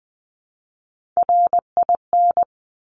Morse code, heard as a keyed tone, about 20 wpm, 700 Hz. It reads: RIN
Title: Top Three Letter Combinations in Words 20wpm